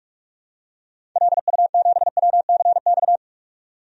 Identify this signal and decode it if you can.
Morse code, heard as a keyed tone, about 44 wpm, 700 Hz. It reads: LU7WCX
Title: DXCC-100 Call Signs - 6 Characters - 45wpm